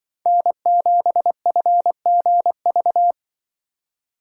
Morse code, heard as a keyed tone, about 24 wpm, 700 Hz. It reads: N7FGV
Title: Speed Racing - USA Call Signs 20wpm